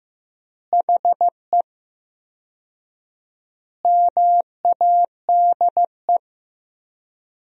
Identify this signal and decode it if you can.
Morse code, heard as a keyed tone, about 15 wpm, 700 Hz. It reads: HE MADE
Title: 4x Word-Spacing - Sentences from Top 300 Words - 15wpm